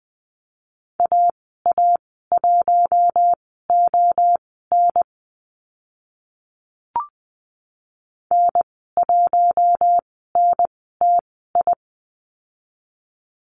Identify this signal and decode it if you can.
Morse code, heard as a keyed tone, about 20 wpm, spaced as for 15 wpm, 700 Hz. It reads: AA1ON N1NTI
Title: 082 Lesson 14g - Number 2 - Call Signs 20wpm